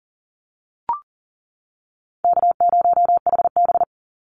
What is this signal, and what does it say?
Morse code, heard as a keyed tone, about 41 wpm, 700 Hz. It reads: K0HB
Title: Easy Call Signs - 40wpm